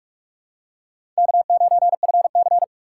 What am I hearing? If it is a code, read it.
Morse code, heard as a keyed tone, about 41 wpm, 700 Hz. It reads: K9FC
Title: USA Call Signs - 4 Characters - 3x Character Spacing - 45wpm